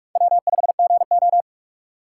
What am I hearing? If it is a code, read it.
Morse code, heard as a keyed tone, about 59 wpm, spaced as for 52 wpm, 700 Hz. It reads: W5GO
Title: Speed Racing - USA Call Signs 45wpm